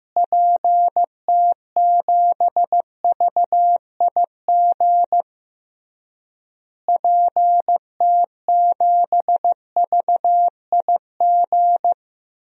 Morse code, heard as a keyed tone, about 16 wpm, 700 Hz. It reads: PT7VIG PT7VIG
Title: DXCC-100 Call Signs - 6 Characters - Repeated 3x - 15wpm